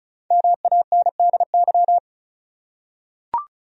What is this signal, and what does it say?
Morse code, heard as a keyed tone, about 36 wpm, 700 Hz. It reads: MANDY
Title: USA Common Names - Max 5 Letters - 35wpm